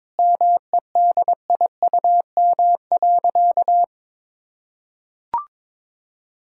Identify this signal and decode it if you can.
Morse code, heard as a keyed tone, about 22 wpm, 700 Hz. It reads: MEDIUM.
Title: Sentences from Top 1000 Words - Review Part 6 - 22wpm